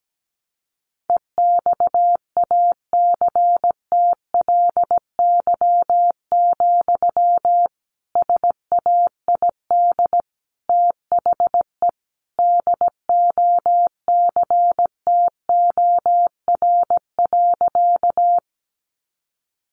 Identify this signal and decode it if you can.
Morse code, heard as a keyed tone, about 17 wpm, 700 Hz. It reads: EXACTLY, SAID THE DOCTOR.
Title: Sentences from Top 1500 Words 17wpm